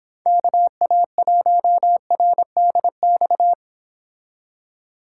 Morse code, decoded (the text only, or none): KA1RDX